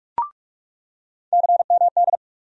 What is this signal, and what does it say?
Morse code, heard as a keyed tone, about 48 wpm, 700 Hz. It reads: CMD
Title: USA Call Sign Postfixes - 3 Letters 45wpm